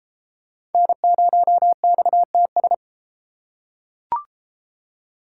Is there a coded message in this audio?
Yes: Morse code from 0.7 to 2.7 s, heard as a keyed tone, about 32 wpm, 700 Hz. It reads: N0XTS